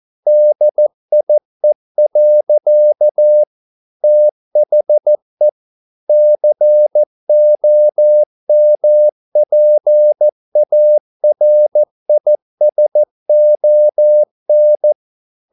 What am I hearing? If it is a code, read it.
Morse code, heard as a keyed tone, about 14 wpm, 600 Hz. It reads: DIE. THE COMPARISON